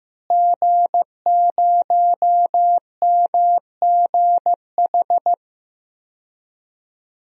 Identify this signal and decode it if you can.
Morse code, heard as a keyed tone, about 15 wpm, 700 Hz. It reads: G0MGH
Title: DXCC-100 Call Signs - 5 Characters - Repeated 2x - 15wpm